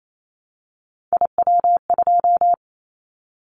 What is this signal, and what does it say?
Morse code, heard as a keyed tone, about 28 wpm, 700 Hz. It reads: IW2
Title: Callsign Prefixes for Top 50 DXCC - 28wpm